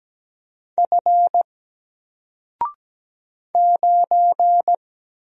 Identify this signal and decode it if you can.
Morse code, heard as a keyed tone, about 17 wpm, 700 Hz. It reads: F 9
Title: Single Letter-Number Alt 17wpm